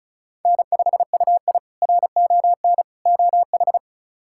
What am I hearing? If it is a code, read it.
Morse code, heard as a keyed tone, about 36 wpm, 700 Hz. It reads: N5UI RON OH
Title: K1USN SST QSO Practice - 35wpm